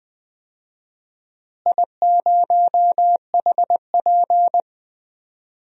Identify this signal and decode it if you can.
Morse code, heard as a keyed tone, about 20 wpm, 700 Hz. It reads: I0HP
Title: DXCC-100 Call Signs - 4 Characters - Repeated 3x - 20wpm